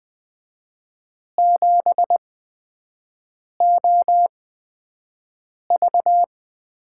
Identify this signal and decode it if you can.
Morse code, heard as a keyed tone, about 20 wpm, spaced as for 5 wpm, 700 Hz. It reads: 7OV
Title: USA Call Signs - 8x Character Spacing - 20wpm